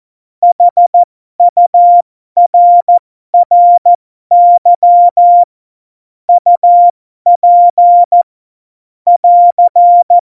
Morse code, heard as a keyed tone, about 14 wpm, spaced as for 12 wpm, 700 Hz. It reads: HURRY UP <AR>
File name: Morsekurs / Witze